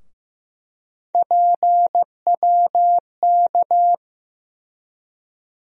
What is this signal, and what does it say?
Morse code, heard as a keyed tone, about 15 wpm, 700 Hz. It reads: PWK
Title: USA Call Sign Postfixes - 3 Letters 15wpm